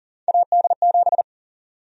Morse code, heard as a keyed tone, about 40 wpm, 700 Hz. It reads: AD7
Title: USA Call Sign Prefixes - 40wpm